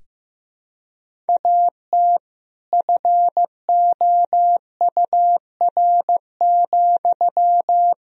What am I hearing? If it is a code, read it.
Morse code, heard as a keyed tone, about 15 wpm, 700 Hz. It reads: AT FOUR,
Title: Sentences from Top 300 Words and Abbreviations 15wpm